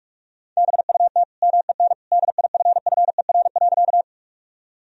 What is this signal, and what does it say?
Morse code, heard as a keyed tone, about 46 wpm, 700 Hz. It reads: BUT MEN DIFFER.
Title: Sentences from Top 1000 Words - Review Part 1 - 45wpm